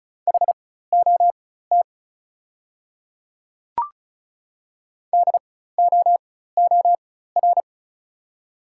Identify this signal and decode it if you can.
Morse code, heard as a keyed tone, about 36 wpm, spaced as for 16 wpm, 690 Hz. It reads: HOT DOOR
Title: 4x Character Spacing - Top 300 Words - 35wpm